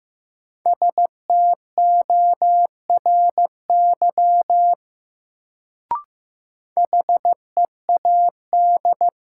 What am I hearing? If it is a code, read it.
Morse code, heard as a keyed tone, about 15 wpm, 700 Hz. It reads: STORY HEAD
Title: Sentences from Top 300 Words - New Word Drill - 15wpm